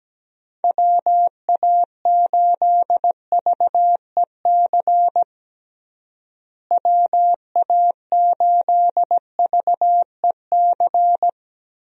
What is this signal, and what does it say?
Morse code, heard as a keyed tone, about 17 wpm, 700 Hz. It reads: WA8VEC WA8VEC
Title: USA Call Signs - 6 Characters - Repeated 2x - 17wpm